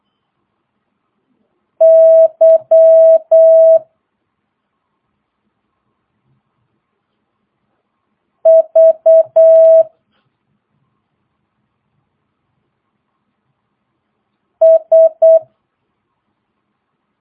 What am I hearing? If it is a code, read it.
Morse code, heard as a keyed tone, about 8 wpm, 650 Hz. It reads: Y V S